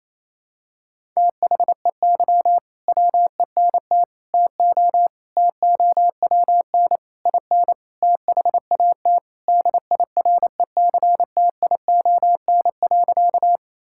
Morse code, heard as a keyed tone, about 27 wpm, 700 Hz. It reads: THEY WENT TO TOWN IN THAT DIRECTION.
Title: Sentences from Top 1000 Words - Review Part 5 - 28wpm